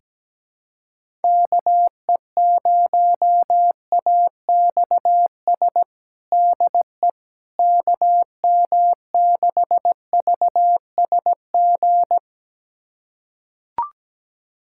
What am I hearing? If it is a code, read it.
Morse code, heard as a keyed tone, about 17 wpm, 700 Hz. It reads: KE0AXS DE KM6VSG
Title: USA Callsign DE Callsign 17wpm